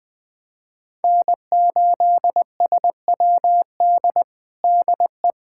Morse code, heard as a keyed tone, about 20 wpm, 700 Hz. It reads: N8SWD DE
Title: USA Callsign DE Callsign 20wpm